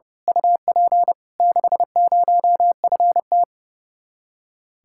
Morse code, heard as a keyed tone, about 28 wpm, 700 Hz. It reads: UP 60FT
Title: Speed-Racing Common QSO Phrases - 30wpm